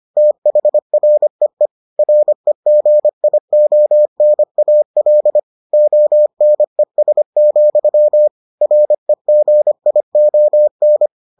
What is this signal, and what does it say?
Morse code, heard as a keyed tone, about 25 wpm, 600 Hz. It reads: THREE REGIONAL ONES, REGION